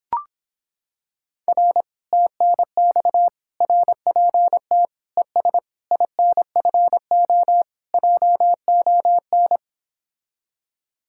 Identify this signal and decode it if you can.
Morse code, heard as a keyed tone, about 26 wpm, 700 Hz. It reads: R TNX RPT ES INFO JON